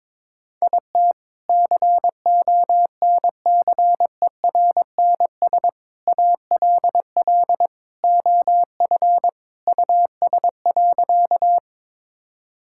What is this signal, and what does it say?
Morse code, heard as a keyed tone, about 22 wpm, 700 Hz. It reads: IT CONCERNS ALL OF US.
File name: practice / 22wpm / Sentences with 3-Word Phrases - 22wpm